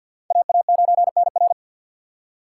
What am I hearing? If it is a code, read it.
Morse code, heard as a keyed tone, about 52 wpm, 700 Hz. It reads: AA9NR
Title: USA Call Signs - 5 Characters - 3x Character Spacing - 50wpm